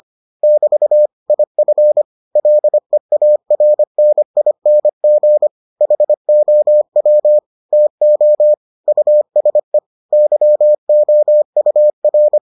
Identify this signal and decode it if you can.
Morse code, heard as a keyed tone, about 25 wpm, 600 Hz. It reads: <BT> IF LEARNING HOW TO USE YOUR